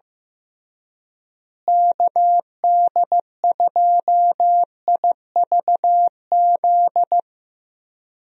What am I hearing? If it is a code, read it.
Morse code, heard as a keyed tone, about 15 wpm, 700 Hz. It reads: KD2IVZ
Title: USA Call Signs - 6 Characters - Repeated 2x - 15wpm